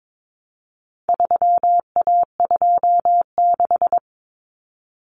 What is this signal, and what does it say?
Morse code, heard as a keyed tone, about 22 wpm, 700 Hz. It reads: 3A26
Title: Common Cut Numbers - 4-Digit 22wpm